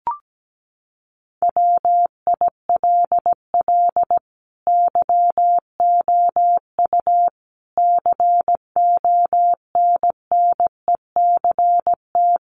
Morse code, heard as a keyed tone, about 17 wpm, 700 Hz. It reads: WILL YOU CONNECT
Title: Sentences from Top 1500 Words 17wpm